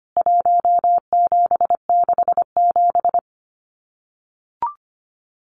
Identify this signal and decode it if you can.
Morse code, heard as a keyed tone, about 25 wpm, 700 Hz. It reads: 1767
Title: CWT Exchange 25wpm